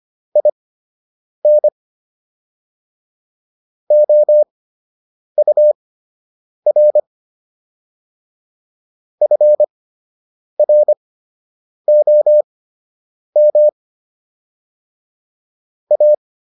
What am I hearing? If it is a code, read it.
Morse code, heard as a keyed tone, about 25 wpm, spaced as for 8 wpm, 600 Hz. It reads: IN OUR FROM A
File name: WD104 08